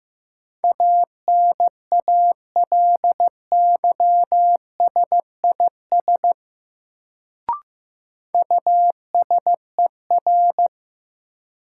Morse code, heard as a keyed tone, about 15 wpm, 700 Hz. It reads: ANALYSIS USER
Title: Sentences from Top 1000 Words - Review Part 5 - New Word Drill - 15wpm